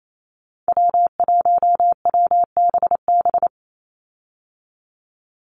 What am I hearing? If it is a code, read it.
Morse code, heard as a keyed tone, about 28 wpm, 700 Hz. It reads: W1WBB
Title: Call Signs Repeated 2x - 28wpm